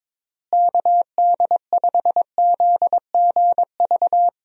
Morse code, heard as a keyed tone, about 22 wpm, 700 Hz. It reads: KD5ZGV
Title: USA Call Signs - 6 Characters - Repeated 2x - 22wpm